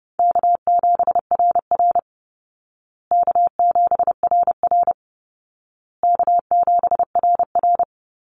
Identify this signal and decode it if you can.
Morse code, heard as a keyed tone, about 30 wpm, 700 Hz. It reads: K7RR K7RR K7RR
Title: Easy Call Signs Repeated 3x - 30wpm